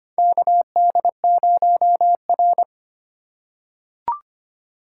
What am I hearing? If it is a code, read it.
Morse code, heard as a keyed tone, about 25 wpm, 700 Hz. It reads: KD0R
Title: USA Call Signs - 4 Characters - 2x Character Spacing - 25wpm